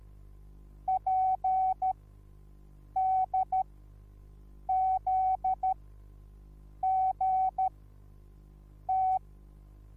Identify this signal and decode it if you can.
Morse code, heard as a keyed tone, about 13 wpm, spaced as for 6 wpm, 760 Hz. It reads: PDZGT